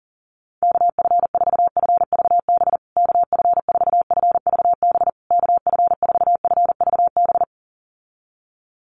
Morse code, heard as a keyed tone, about 40 wpm, 700 Hz. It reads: KF4FVB KF4FVB KF4FVB
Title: USA Call Signs Repeated 3x 40wpm